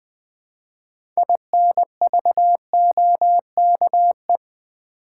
Morse code, heard as a keyed tone, about 20 wpm, 700 Hz. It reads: INVOKE
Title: Build A Word - 3 Letters At A Time - 6-Letter Words - 20wpm